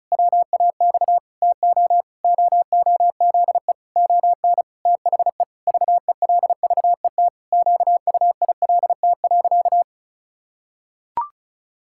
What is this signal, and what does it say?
Morse code, heard as a keyed tone, about 36 wpm, 700 Hz. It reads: WAX TO OOZE ON THE VELVET QUILT.
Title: Pangram Challenge v2 - 35wpm